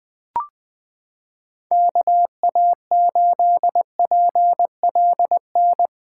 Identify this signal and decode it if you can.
Morse code, heard as a keyed tone, about 21 wpm, 700 Hz. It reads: KA8PLN
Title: USA Callsign DE Callsign 20wpm